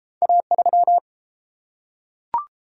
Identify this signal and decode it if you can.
Morse code, heard as a keyed tone, about 30 wpm, 700 Hz. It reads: A3